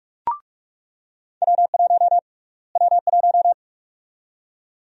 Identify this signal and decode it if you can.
Morse code, heard as a keyed tone, about 46 wpm, 700 Hz. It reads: W1 W1
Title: USA Call Sign Prefixes - Repeated 2x - 45wpm